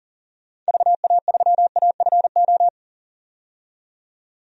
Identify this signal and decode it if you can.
Morse code, heard as a keyed tone, about 40 wpm, 700 Hz. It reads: VA3AFO
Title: DXCC-100 Call Signs - 6 Characters - 40wpm